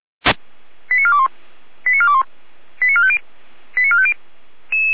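Five-tone selective call (ZVEI), heard as quick runs of five tones, tones of about 70 ms. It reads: 89421 89421 89450 89450
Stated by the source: Five-tone ZVEI selective call received on SDR.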